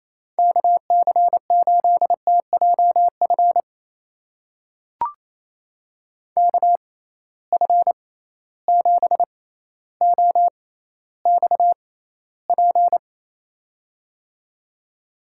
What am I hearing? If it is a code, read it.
Morse code, heard as a keyed tone, about 28 wpm, spaced as for 14 wpm, 700 Hz. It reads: KC8TJF KF7OXP
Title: USA Call Signs - 6 Characters - 6x Character Spacing - 28wpm